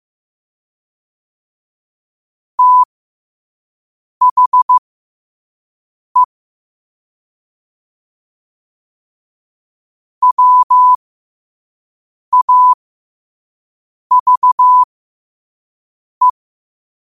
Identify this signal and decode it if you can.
Morse code, heard as a keyed tone, about 15 wpm, spaced as for 5 wpm, 1000 Hz. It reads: THE WAVE
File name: qotdfiles / 5wpm